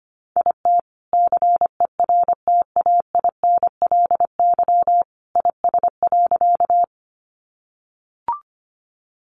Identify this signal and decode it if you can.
Morse code, heard as a keyed tone, about 25 wpm, 700 Hz. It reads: IT CERTAINLY IS.
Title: Sentences from Top 900 Words 25wpm